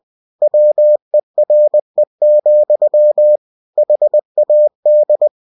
Morse code, heard as a keyed tone, about 20 wpm, 600 Hz. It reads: WERE, HAD
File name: cw-books / winnie-the-pooh / winnie-20wpm-page-0022